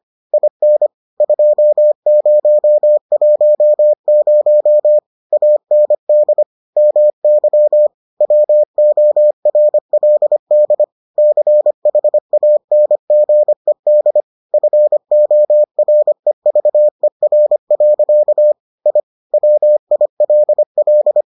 Morse code, heard as a keyed tone, about 25 wpm, 600 Hz. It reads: IN 2010 AND MY WORLD CHANGED FOREVER. I WILL